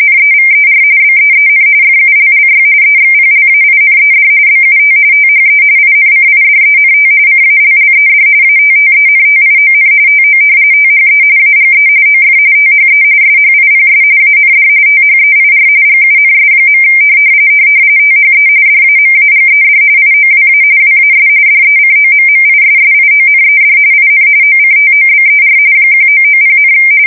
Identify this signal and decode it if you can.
RTTY, about 45.45 baud. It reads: NEVER BEEN FULLY EXPLAINED BUT THE INCIDENT CALLED ATTENTION TO THE AGE OF THE SHIPS AND THEIR GUNS. THE DAMAGED IOWA WAS DECOMMISSIONED IN OCTOBER 1990,
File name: fsk170 / battleshipnj